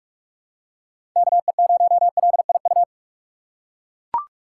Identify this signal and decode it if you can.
Morse code, heard as a keyed tone, about 44 wpm, 700 Hz. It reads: KE0LIU
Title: USA Call Signs - 6 Characters - 8x Character Spacing - 45wpm